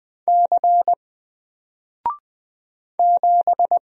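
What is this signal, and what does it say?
Morse code, heard as a keyed tone, about 20 wpm, 700 Hz. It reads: C 7
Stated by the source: Single letter and number Morse drill at 20 wpm, from N4LCD.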